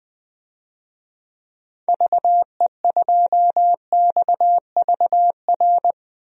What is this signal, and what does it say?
Morse code, heard as a keyed tone, about 21 wpm, 700 Hz. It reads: VE2XVR
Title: DXCC-100 Call Signs - 6 Characters - 20wpm